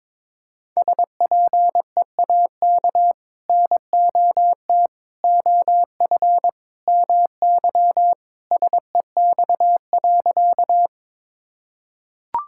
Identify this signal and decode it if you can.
Morse code, heard as a keyed tone, about 22 wpm, 700 Hz. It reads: SPEAK NOT OF MY SEX.
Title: Sentences from Top 1600 Words 22wpm